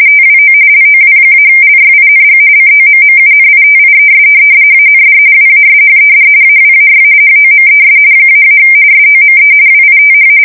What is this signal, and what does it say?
RTTY, about 45.45 baud. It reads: TWO LOW-LYING COMMUNITIES AND WASHED AWAY HOMES -- SOME WITH
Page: 170 Shift 2125/2295